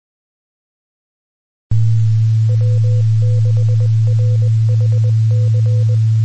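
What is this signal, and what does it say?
Morse code, heard as a keyed tone, about 20 wpm, 500 Hz, with noise. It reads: W6RHC
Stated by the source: Recorded on 2 m band, 146.850 MHz.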